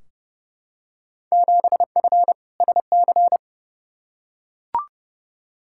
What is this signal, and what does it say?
Morse code, heard as a keyed tone, about 32 wpm, 700 Hz. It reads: 7F SC
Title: Field Day Exchanges Encore - 30wpm